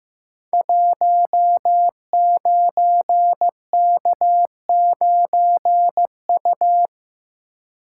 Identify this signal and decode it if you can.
Morse code, heard as a keyed tone, about 15 wpm, 700 Hz. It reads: 19K9U
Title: Code Groups - 5 Letter:Numbers - 15wpm